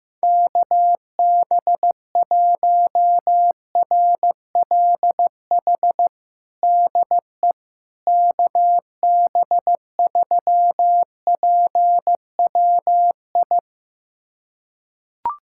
Morse code, heard as a keyed tone, about 15 wpm, 700 Hz. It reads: KB1RLH DE KB3PWI
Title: USA Callsign DE Callsign 15wpm